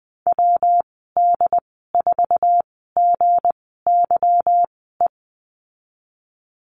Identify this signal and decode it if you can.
Morse code, heard as a keyed tone, about 20 wpm, spaced as for 15 wpm, 700 Hz. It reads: WD4GYE
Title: 173 Lesson 28g - Number 9 - Call Signs 20wpm